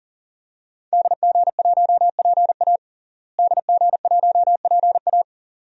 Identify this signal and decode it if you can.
Morse code, heard as a keyed tone, about 37 wpm, 700 Hz. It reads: DG1PA DG1PA
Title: DXCC-100 Call Signs - 5 Characters - Repeated 2x - 40wpm